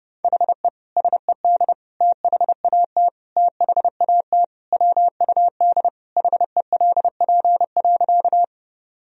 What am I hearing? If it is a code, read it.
Morse code, heard as a keyed tone, about 32 wpm, 700 Hz. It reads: HE SED THAT THAT WUD HELP.
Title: Sentences from Top 200 Words and Abbreviations 30wpm